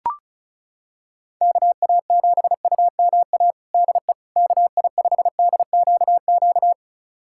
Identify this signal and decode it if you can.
Morse code, heard as a keyed tone, about 36 wpm, 700 Hz. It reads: KA7UMA DE KI5DQQ
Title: USA Callsign DE Callsign - 6 Characters - 35wpm